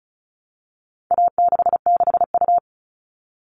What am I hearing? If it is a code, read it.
Morse code, heard as a keyed tone, about 34 wpm, 700 Hz. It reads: A66U
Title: All Cut Numbers - 4-Digit 35wpm